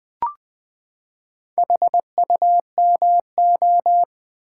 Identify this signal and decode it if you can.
Morse code, heard as a keyed tone, about 20 wpm, 700 Hz. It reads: HUMO
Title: Build A Word - 4 Letters At A Time - 8-Letter Words - 20wpm